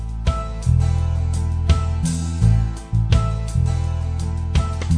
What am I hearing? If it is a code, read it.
There is no code.